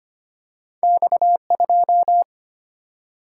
Morse code, heard as a keyed tone, about 25 wpm, 700 Hz. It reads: X2
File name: Two Letter-Number Combinations - 25wpm